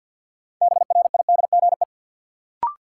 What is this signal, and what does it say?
Morse code, heard as a keyed tone, about 50 wpm, 700 Hz. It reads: BRIDGE
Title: Sentences from Top 1200 Words - New Word Drill Alt - 50wpm